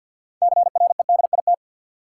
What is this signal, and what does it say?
Morse code, heard as a keyed tone, about 50 wpm, 700 Hz. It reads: CREDIT
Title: Sentences from Top 1000 Words - New Word Drill - 50wpm